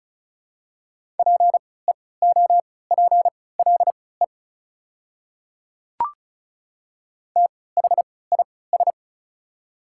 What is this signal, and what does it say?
Morse code, heard as a keyed tone, about 34 wpm, spaced as for 20 wpm, 690 Hz. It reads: PEOPLE THIS
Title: 3x Character Spacing - Top 100 Words - 35wpm